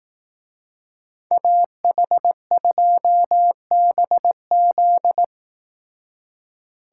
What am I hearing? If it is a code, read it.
Morse code, heard as a keyed tone, about 18 wpm, 700 Hz. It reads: AH2BZ
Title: Speed Racing - USA 2x2 Call Signs 15wpm